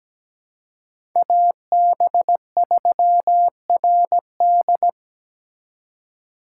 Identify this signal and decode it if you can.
Morse code, heard as a keyed tone, about 20 wpm, 700 Hz. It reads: AB3RD
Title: Speed Racing - USA 2x2 Call Signs 17wpm